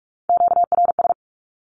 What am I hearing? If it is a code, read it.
Morse code, heard as a keyed tone, about 44 wpm, 700 Hz. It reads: QRS
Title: Q-Codes 45wpm